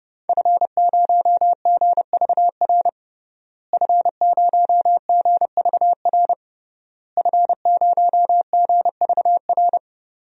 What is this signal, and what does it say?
Morse code, heard as a keyed tone, about 29 wpm, 700 Hz. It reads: F0GVR F0GVR F0GVR
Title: DXCC-100 Call Signs - 5 Characters - Repeated 3x - 30wpm